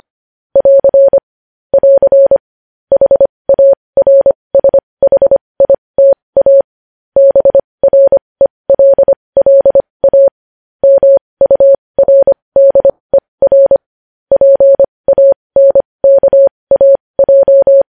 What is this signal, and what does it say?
Morse code, heard as a keyed tone, about 25 wpm, spaced as for 20 wpm, 550 Hz. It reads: <AR> <AR> HARSHITA BRELLA MURDER PANKAJ
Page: news-headlines-in-morse-code-at-20-wpm